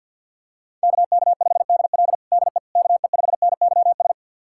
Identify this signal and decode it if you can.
Morse code, heard as a keyed tone, about 53 wpm, 700 Hz. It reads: KK5DL DE KE5NYS
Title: USA Callsign DE Callsign 50wpm